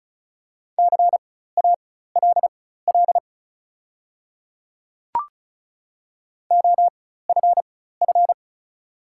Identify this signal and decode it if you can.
Morse code, heard as a keyed tone, about 34 wpm, spaced as for 16 wpm, 700 Hz. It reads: CALL OFF
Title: 4x Character Spacing - Top 200 Words - 35wpm